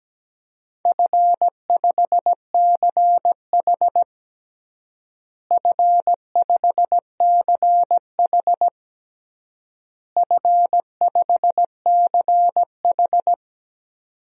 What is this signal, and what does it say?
Morse code, heard as a keyed tone, about 17 wpm, 700 Hz. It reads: F5CH F5CH F5CH
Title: DXCC-100 Call Signs - 4 Characters - Repeated 3x - 17wpm